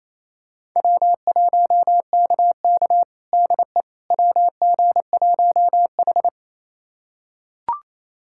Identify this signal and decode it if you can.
Morse code, heard as a keyed tone, about 28 wpm, 700 Hz. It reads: W1KK DE WG1H
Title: USA Callsign DE Callsign - 4 Characters - 28wpm